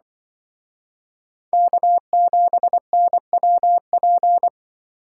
Morse code, heard as a keyed tone, about 23 wpm, 700 Hz. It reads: K7NWP
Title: Speed Racing - USA 1x3 Call Signs 20wpm